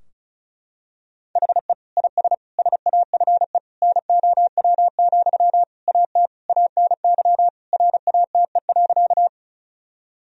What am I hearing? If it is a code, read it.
Morse code, heard as a keyed tone, about 36 wpm, 700 Hz. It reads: HE IS SAFE NOW, AT ANY RATE.
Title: Sentences from Top 1000 Words - Review Part 4 - 35wpm